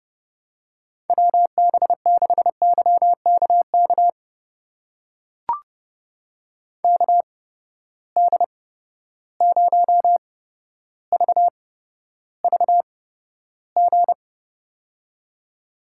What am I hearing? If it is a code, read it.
Morse code, heard as a keyed tone, about 31 wpm, spaced as for 13 wpm, 700 Hz. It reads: WB6YKK KD0VVG
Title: USA Call Signs - 6 Characters - 8x Character Spacing - 30wpm